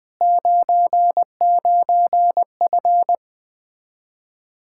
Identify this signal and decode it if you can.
Morse code, heard as a keyed tone, about 20 wpm, 700 Hz. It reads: 99F
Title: Speed-Racing Common QSO Phrases Encore - 17wpm